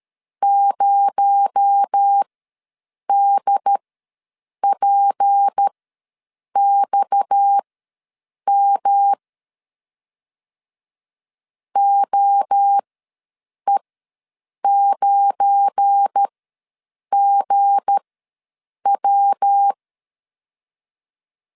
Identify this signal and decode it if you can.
Morse code, heard as a keyed tone, about 13 wpm, spaced as for 7 wpm, 800 Hz. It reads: 0DPXM OE9GW
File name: LEZIONI / 7 GF9Z / 7 GF9 ws 12wpm 3344